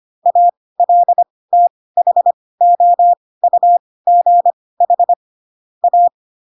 Morse code, heard as a keyed tone, about 25 wpm, spaced as for 18 wpm, 700 Hz. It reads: ALTHOUGH A